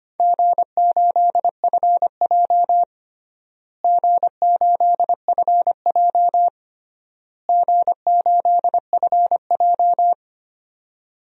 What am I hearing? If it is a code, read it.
Morse code, heard as a keyed tone, about 25 wpm, 700 Hz. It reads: G8FJ G8FJ G8FJ
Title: DXCC-100 Call Signs - 4 Characters - Repeated 3x - 25wpm